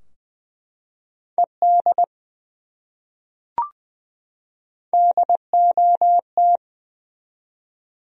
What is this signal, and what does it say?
Morse code, heard as a keyed tone, about 20 wpm, 700 Hz. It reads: ED DOT
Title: USA Common Names - Max 3 Letters - 20wpm